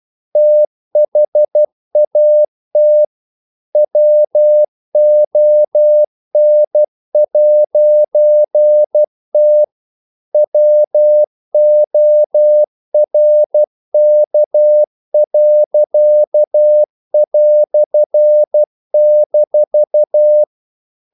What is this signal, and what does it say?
Morse code, heard as a keyed tone, about 12 wpm, 600 Hz. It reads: THAT WON'T WORK."-
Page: English quotations in CW (telegraphy - morse)